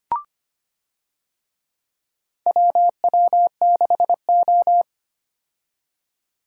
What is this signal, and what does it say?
Morse code, heard as a keyed tone, about 25 wpm, 700 Hz. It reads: WW6O
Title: USA Call Signs - 4 Characters - 25wpm